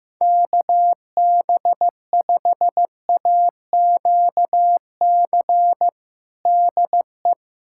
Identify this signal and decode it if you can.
Morse code, heard as a keyed tone, about 15 wpm, 700 Hz. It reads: KB5AQC DE
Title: USA Callsign DE Callsign - 6 Characters - 15wpm